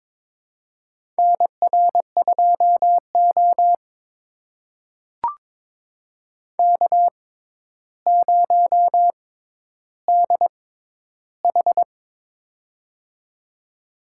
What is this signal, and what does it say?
Morse code, heard as a keyed tone, about 22 wpm, spaced as for 11 wpm, 700 Hz. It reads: NR2O K0DH
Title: USA Call Signs - 4 Characters - 6x Character Spacing - 22wpm